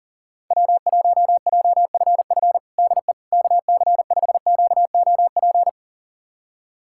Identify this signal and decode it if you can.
Morse code, heard as a keyed tone, about 38 wpm, 700 Hz. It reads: W1JFF DE KC5QOP
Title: USA Callsign DE Callsign 40wpm